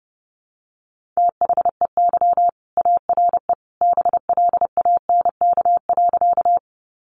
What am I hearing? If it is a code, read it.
Morse code, heard as a keyed tone, about 30 wpm, 700 Hz. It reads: THEY ARE BLANK.
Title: Sentences from Top 1500 Words 30wpm